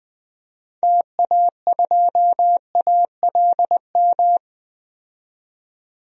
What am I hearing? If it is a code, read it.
Morse code, heard as a keyed tone, about 20 wpm, 700 Hz. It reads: TA2ALM
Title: DXCC-100 Call Signs - 6 Characters - 20wpm